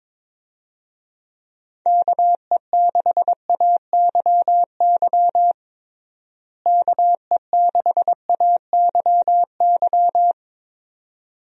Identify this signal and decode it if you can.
Morse code, heard as a keyed tone, about 22 wpm, 700 Hz. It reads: KE6AYY KE6AYY
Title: USA Call Signs - 6 Characters - Repeated 2x - 22wpm